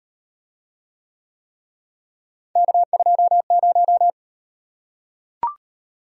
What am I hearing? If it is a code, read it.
Morse code, heard as a keyed tone, about 37 wpm, 700 Hz. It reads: K20